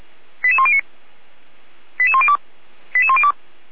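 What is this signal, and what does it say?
Five-tone selective call (ZVEI), heard as quick runs of five tones, tones of about 70 ms. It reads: 88109 88102 88102
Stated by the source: ZVEI five-tone call received by SDR, logged by the Orlamünde fire brigade (feuerwehr-orlamuende).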